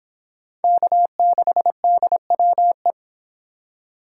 Morse code, heard as a keyed tone, about 25 wpm, 700 Hz. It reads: K6DWE